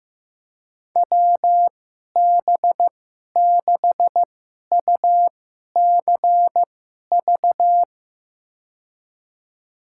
Morse code, heard as a keyed tone, about 15 wpm, spaced as for 11 wpm, 700 Hz. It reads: WB6UCV
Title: USA Call Signs - 2x Character Spacing - 15wpm